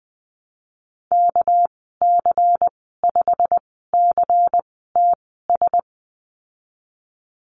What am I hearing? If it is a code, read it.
Morse code, heard as a keyed tone, about 20 wpm, spaced as for 15 wpm, 700 Hz. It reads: KC5CTS
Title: 200 Lesson 32g - Letter k - Call Signs 20wpm